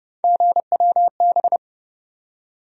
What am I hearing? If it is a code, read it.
Morse code, heard as a keyed tone, about 30 wpm, 700 Hz. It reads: GWB
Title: USA Call Sign Postfixes - 3 Letters 30wpm